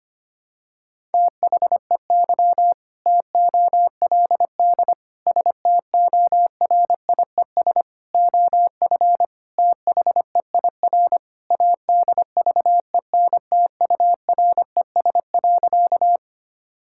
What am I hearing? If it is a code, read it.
Morse code, heard as a keyed tone, about 25 wpm, 700 Hz. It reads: THEY TOLD STORIES OF THEIR ADVENTURES.
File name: practice / 25wpm / Sentences with 2-Word Phrases - 25wpm